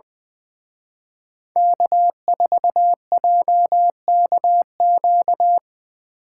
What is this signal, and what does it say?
Morse code, heard as a keyed tone, about 24 wpm, 700 Hz. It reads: K4JKQ
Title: Speed Racing - USA 1x3 Call Signs 20wpm